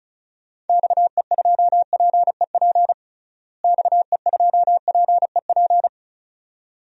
Morse code, heard as a keyed tone, about 36 wpm, 700 Hz. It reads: XE2PEP XE2PEP
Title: DXCC-100 Call Signs - 6 Characters - Repeated 2x - 35wpm